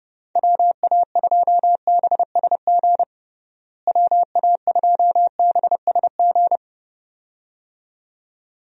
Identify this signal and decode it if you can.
Morse code, heard as a keyed tone, about 30 wpm, 700 Hz. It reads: WA2BSG WA2BSG
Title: USA Call Signs - 6 Characters - Repeated 2x - 30wpm